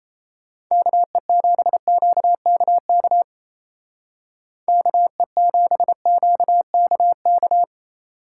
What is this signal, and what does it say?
Morse code, heard as a keyed tone, about 37 wpm, 700 Hz. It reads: KE7QKK KE7QKK